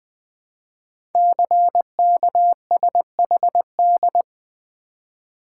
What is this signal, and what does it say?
Morse code, heard as a keyed tone, about 20 wpm, 700 Hz. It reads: CKSHD
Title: Code Groups - 5 Letters - 20wpm